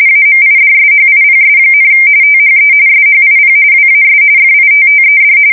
RTTY, about 45.45 baud. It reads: AMIDSHIPS, THE SHIP WAS FITTED
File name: fsk170 / USS-New-Jersey